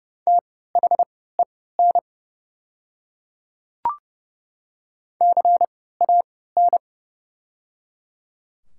Morse code, heard as a keyed tone, about 30 wpm, spaced as for 17 wpm, 700 Hz. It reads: THEN CAN
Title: 3x Character Spacing - Top 100 Words - 30wpm